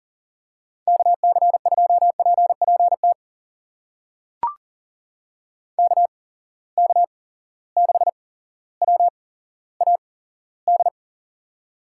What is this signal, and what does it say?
Morse code, heard as a keyed tone, about 40 wpm, spaced as for 17 wpm, 700 Hz. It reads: KC2PPT KK6WAD
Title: USA Call Signs - 6 Characters - 8x Character Spacing - 40wpm